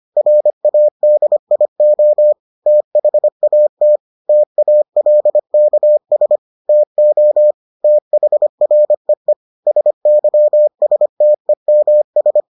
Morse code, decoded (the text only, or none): RADIO THAT TALKS TO THREE SYSTEMS